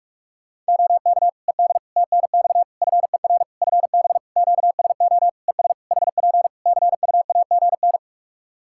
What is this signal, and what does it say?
Morse code, heard as a keyed tone, about 41 wpm, 700 Hz. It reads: OK ED TNX FER FB QSO ES HP CUAGN
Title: Practice QSO - Exchange 4 - Ending - 45wpm